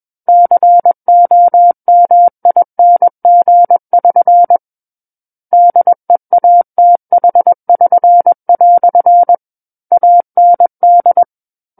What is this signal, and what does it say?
Morse code, heard as a keyed tone, about 21 wpm, 700 Hz. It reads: COMING<SN> DEATH<SN>" AND
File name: Book One / 12-Chapter 12 - 21 WPM